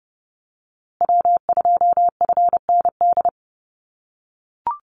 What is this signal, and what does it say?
Morse code, heard as a keyed tone, about 30 wpm, 700 Hz. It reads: W2FND
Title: USA Call Signs Repeated 2x 30wpm